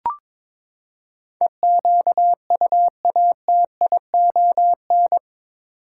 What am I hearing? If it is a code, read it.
Morse code, heard as a keyed tone, about 22 wpm, 700 Hz. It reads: EQUATION
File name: practice / 22wpm / Eight Letter Words - 22wpm